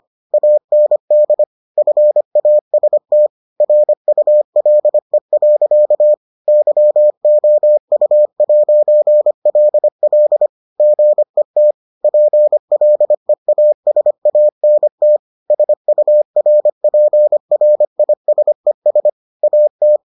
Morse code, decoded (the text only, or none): AND FAST RULE. YOU'LL GET PLEASANT SURPRISES AT